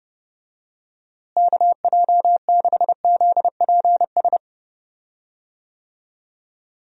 Morse code, heard as a keyed tone, about 30 wpm, 700 Hz. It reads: KJ6ZPS
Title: USA Call Signs - 6 Characters - Repeated 2x - 30wpm